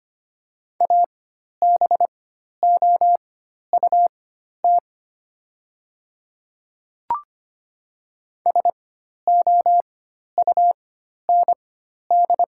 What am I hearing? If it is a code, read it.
Morse code, heard as a keyed tone, about 25 wpm, spaced as for 12 wpm, 700 Hz. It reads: ABOUT SOUND
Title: 4x Character Spacing - Top 200 Words - 25wpm